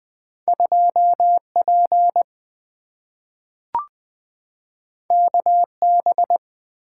Morse code, heard as a keyed tone, about 20 wpm, 700 Hz. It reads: 2 P K B
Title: Two Letter-Number Combinations - 20wpm